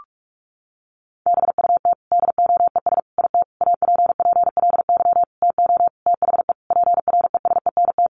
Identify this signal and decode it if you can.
Morse code, heard as a keyed tone, about 44 wpm, 700 Hz. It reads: BUT DOES IT APPLY TO THE PRESENT
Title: Sentences from Top 1200 Words 45wpm